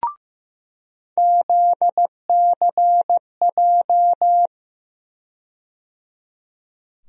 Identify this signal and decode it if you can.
Morse code, heard as a keyed tone, about 15 wpm, 700 Hz. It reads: ZCJ
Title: USA Call Sign Postfixes - 3 Letters 15wpm